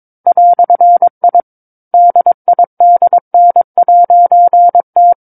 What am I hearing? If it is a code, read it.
Morse code, heard as a keyed tone, about 22 wpm, 700 Hz. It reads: "I DIDN'T
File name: 06-Chapter-06 22 WPM